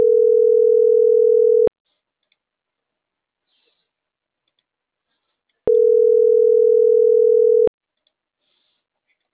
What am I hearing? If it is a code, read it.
Ringback tone (North America), 440 + 480 Hz.